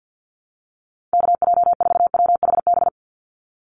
Morse code, heard as a keyed tone, about 48 wpm, 700 Hz. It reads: KJ4WHB
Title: USA Call Signs 50wpm